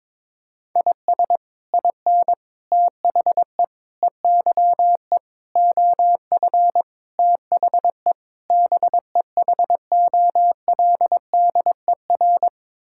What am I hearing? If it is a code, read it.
Morse code, heard as a keyed tone, about 22 wpm, 700 Hz. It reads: IS IN THE EYE OF THE BEHOLDER
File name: practice / 22wpm / Sets of 7 Words - 22wpm